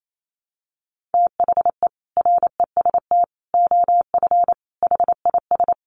Morse code, heard as a keyed tone, about 28 wpm, 700 Hz. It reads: THE REST OF HIS
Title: Sets of 5 Words 28wpm